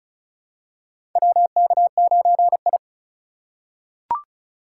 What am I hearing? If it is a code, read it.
Morse code, heard as a keyed tone, about 36 wpm, 700 Hz. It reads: WK9I
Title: USA Call Signs - 4 Characters - 2x Character Spacing - 35wpm